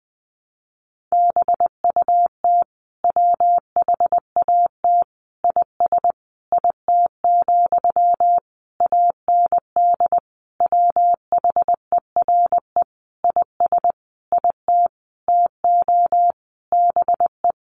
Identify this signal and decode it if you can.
Morse code, heard as a keyed tone, about 20 wpm, 700 Hz. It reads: BUT WHAT IS IT, AND WHERE IS IT TO BE
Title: Sentences from Top 300 Words 20wpm